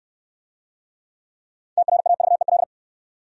Morse code, heard as a keyed tone, about 62 wpm, 700 Hz. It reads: TRAVEL